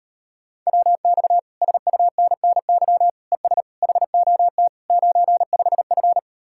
Morse code, heard as a keyed tone, about 38 wpm, 700 Hz. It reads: WX SUNNY ES HOT 95F